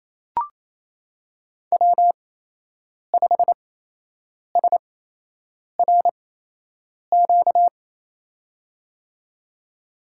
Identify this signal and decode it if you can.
Morse code, heard as a keyed tone, about 28 wpm, spaced as for 8 wpm, 700 Hz. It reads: W5SRQ
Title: USA Call Signs - 8x Character Spacing - 28wpm